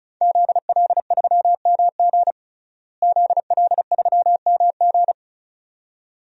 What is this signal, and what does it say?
Morse code, heard as a keyed tone, about 36 wpm, 700 Hz. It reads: ZL3MG ZL3MG
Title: DXCC-100 Call Signs - 5 Characters - Repeated 2x - 35wpm